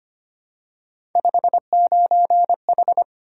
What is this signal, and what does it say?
Morse code, heard as a keyed tone, about 25 wpm, 700 Hz. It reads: 59H
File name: Three Letter-Number Combinations - 25wpm